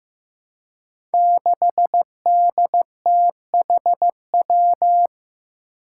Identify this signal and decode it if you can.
Morse code, heard as a keyed tone, about 15 wpm, 700 Hz. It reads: 6DTHW
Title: Five Challenging Letter-Number Combinations 15wpm